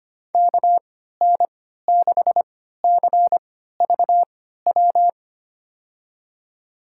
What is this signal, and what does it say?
Morse code, heard as a keyed tone, about 25 wpm, spaced as for 14 wpm, 690 Hz. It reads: KN6CVW
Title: USA Call Signs - 6 Characters - 3x Character Spacing - 25wpm